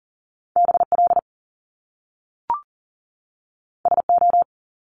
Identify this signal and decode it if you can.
Morse code, heard as a keyed tone, about 41 wpm, 700 Hz. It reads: BL SO
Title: Top Two Letter Combinations in Words 40wpm